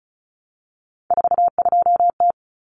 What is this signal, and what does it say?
Morse code, heard as a keyed tone, about 34 wpm, 700 Hz. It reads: 42T
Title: Common Cut Numbers - 3-Digit 35wpm